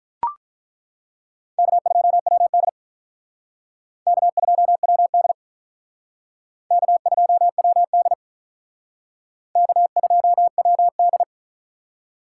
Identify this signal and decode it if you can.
Morse code, heard as a keyed tone, about 42 wpm, 700 Hz. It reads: K2WD K2WD K2WD K2WD
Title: Speed Racing - USA 1x2 Call Signs 35wpm